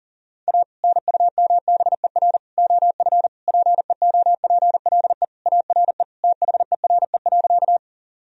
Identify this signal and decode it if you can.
Morse code, heard as a keyed tone, about 40 wpm, 700 Hz. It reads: A NUMBER OF PEOPLE ARE THERE.
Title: Sentences from Top 1000 Words - Review Part 1 - 40wpm